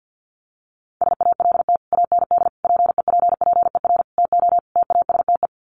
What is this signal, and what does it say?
Morse code, heard as a keyed tone, about 50 wpm, 700 Hz. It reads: SALT AND PEPPER TO TASTE
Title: Sets of 5 Words 50wpm